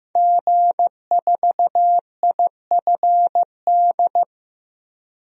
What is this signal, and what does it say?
Morse code, heard as a keyed tone, about 15 wpm, 700 Hz. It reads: G4IFD
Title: DXCC-100 Call Signs - 5 Characters - 15wpm